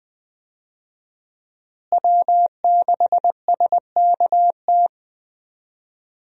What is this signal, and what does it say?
Morse code, heard as a keyed tone, about 21 wpm, 700 Hz. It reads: W6SKT
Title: USA Call Signs - 5 Characters - Repeated 2x - 20wpm